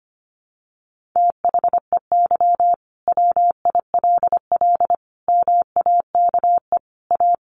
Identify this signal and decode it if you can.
Morse code, heard as a keyed tone, about 25 wpm, 700 Hz. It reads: THEY WILL MAKE A
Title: Sentences from Top 1500 Words 25wpm